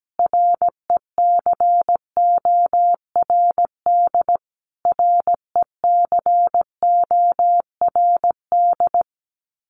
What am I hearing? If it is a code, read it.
Morse code, heard as a keyed tone, about 17 wpm, 700 Hz. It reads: RECORD RECORD
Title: Top 500 Words Repeated 17wpm